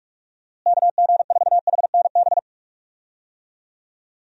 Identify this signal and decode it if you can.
Morse code, heard as a keyed tone, about 46 wpm, 700 Hz. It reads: KG4HNB
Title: USA Call Signs - 6 Characters - Repeated 3x - 45wpm